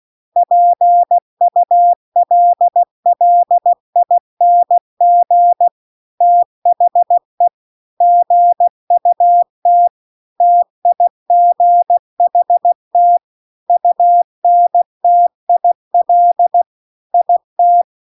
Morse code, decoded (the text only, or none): PULLING THE GUT TIGHT UNTIL IT